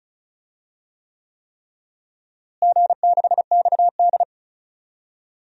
Morse code, heard as a keyed tone, about 36 wpm, 700 Hz. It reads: G6XD
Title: DXCC-100 Call Signs - 4 Characters - Repeated 3x - 35wpm